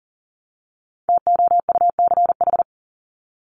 Morse code, heard as a keyed tone, about 40 wpm, 700 Hz. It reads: TOUCH
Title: Top 1001-1250 Words 40wpm